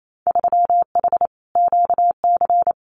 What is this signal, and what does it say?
Morse code, heard as a keyed tone, about 28 wpm, 700 Hz. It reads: 3H QC
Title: Winter Field Day Exchanges Encore 28wpm